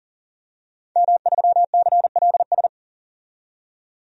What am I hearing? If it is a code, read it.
Morse code, heard as a keyed tone, about 40 wpm, 700 Hz. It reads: M3CLS
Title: DXCC-100 Call Signs - 5 Characters - 40wpm